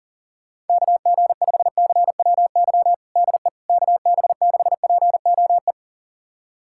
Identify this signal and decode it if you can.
Morse code, heard as a keyed tone, about 40 wpm, 700 Hz. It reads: KG5CWY DE KB6POE
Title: USA Callsign DE Callsign - 6 Characters - 40wpm